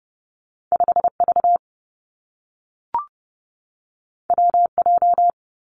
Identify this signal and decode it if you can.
Morse code, heard as a keyed tone, about 30 wpm, 700 Hz. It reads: 5 V W J
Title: Two Challenging Letter-Number Combinations 30wpm